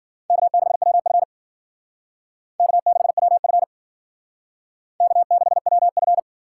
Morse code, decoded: K6WF K6WF K6WF

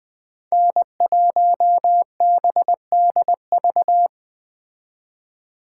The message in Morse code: N1BDV

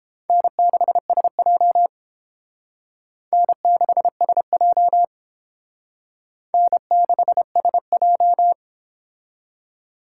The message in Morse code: N6SJ N6SJ N6SJ

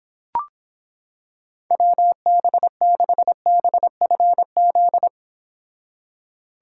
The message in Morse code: WB6BFZ

